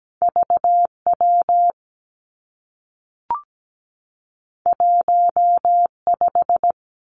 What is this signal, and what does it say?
Morse code, heard as a keyed tone, about 17 wpm, 700 Hz. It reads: VW 15
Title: Two Challenging Letter-Number Combinations 17wpm